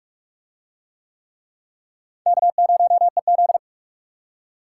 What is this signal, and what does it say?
Morse code, heard as a keyed tone, about 39 wpm, 700 Hz. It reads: K0EZ